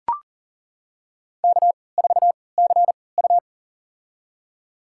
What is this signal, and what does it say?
Morse code, heard as a keyed tone, about 40 wpm, spaced as for 23 wpm, 700 Hz. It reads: K4CU
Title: USA Call Signs - 4 Characters - 3x Character Spacing - 40wpm